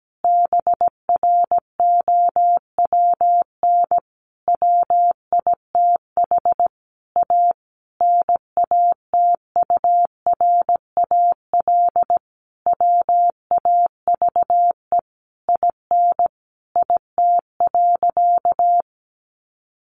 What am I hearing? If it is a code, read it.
Morse code, heard as a keyed tone, about 17 wpm, 700 Hz. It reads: BROWN WITH A NATURAL WAVE IN IT.
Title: Sentences from Top 1400 Words 17wpm